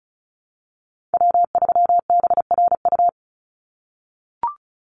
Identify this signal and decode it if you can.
Morse code, heard as a keyed tone, about 34 wpm, 700 Hz. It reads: W3BRU
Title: USA Call Signs 35wpm